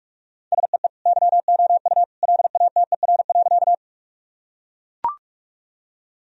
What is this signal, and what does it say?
Morse code, heard as a keyed tone, about 42 wpm, 700 Hz. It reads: SEE YOU LATER.
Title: Sentences from Top 1000 Words - Review Part 4 - 45wpm